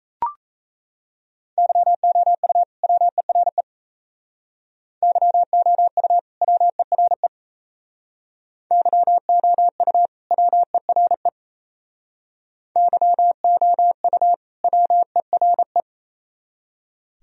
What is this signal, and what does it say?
Morse code, heard as a keyed tone, about 34 wpm, 700 Hz. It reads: YOU WERE YOU WERE YOU WERE YOU WERE
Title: Speed-Racing - Sets of 2 Words 28wpm